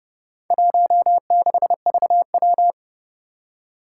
Morse code, heard as a keyed tone, about 30 wpm, 700 Hz. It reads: 16VW
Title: Four Challenging Letter-Number Combinations 30wpm